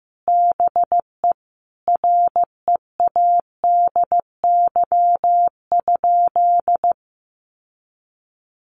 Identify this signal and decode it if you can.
Morse code, heard as a keyed tone, about 15 wpm, 700 Hz. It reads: BE READY?
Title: Sentences from Top 1200 Words 15wpm